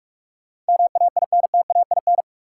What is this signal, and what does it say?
Morse code, heard as a keyed tone, about 41 wpm, 700 Hz. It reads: MAINTAIN
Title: Sentences from Top 1000 Words - Review Part 5 - New Word Drill - 45wpm